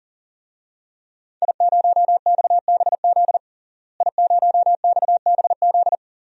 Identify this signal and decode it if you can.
Morse code, heard as a keyed tone, about 37 wpm, 700 Hz. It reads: I0XBZ I0XBZ
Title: DXCC-100 Call Signs - 5 Characters - Repeated 2x - 40wpm